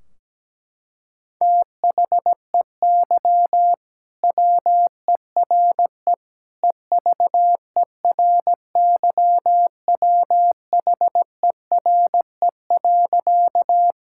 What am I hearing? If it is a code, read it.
Morse code, heard as a keyed tone, about 17 wpm, 700 Hz. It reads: THEY WERE EVERYWHERE.
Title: Sentences from Top 1600 Words 17wpm